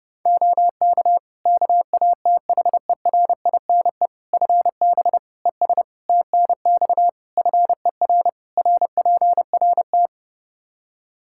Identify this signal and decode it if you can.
Morse code, heard as a keyed tone, about 31 wpm, 700 Hz. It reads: OK KATHERINE FB ES TNX FER RPRT
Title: K1USN SST QSO Practice 30wpm